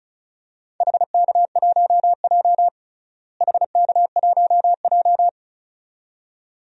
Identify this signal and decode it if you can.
Morse code, heard as a keyed tone, about 36 wpm, 700 Hz. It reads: HK1J HK1J
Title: DXCC-100 Call Signs - 4 Characters - Repeated 2x - 35wpm